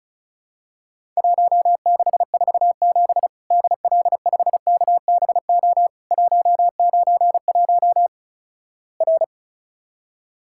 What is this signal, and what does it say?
Morse code, heard as a keyed tone, about 35 wpm, 700 Hz. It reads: 1647 DL5KBO 191 R
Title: Worked All Europe DX Contest - QTC Exchange - 35wpm